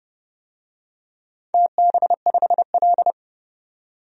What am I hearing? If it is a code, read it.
Morse code, heard as a keyed tone, about 30 wpm, 700 Hz. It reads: TB5L
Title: DXCC-100 Call Signs - 4 Characters - 30wpm